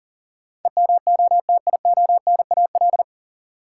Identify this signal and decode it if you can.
Morse code, heard as a keyed tone, about 40 wpm, 700 Hz. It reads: EMOTIONAL
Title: Sentences from Top 1000 Words - Review Part 5 - New Word Drill - 40wpm